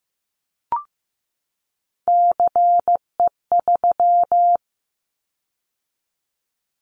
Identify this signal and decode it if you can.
Morse code, heard as a keyed tone, about 15 wpm, 700 Hz. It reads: CE3
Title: Callsign Prefixes for Top 100 DXCC - 15wpm